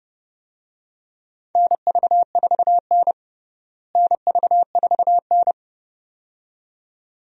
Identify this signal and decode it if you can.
Morse code, heard as a keyed tone, about 32 wpm, 700 Hz. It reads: NV4N NV4N
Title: USA Call Signs - 4 Characters - Repeated 2x - 30wpm